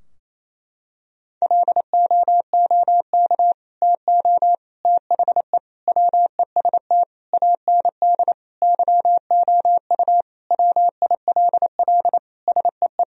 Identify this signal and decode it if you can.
Morse code, heard as a keyed tone, about 27 wpm, 700 Hz. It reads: LOOK TO THE WEST AND YOU WILL SEE
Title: Sentences from Top 1000 Words - Review Part 2 - 28wpm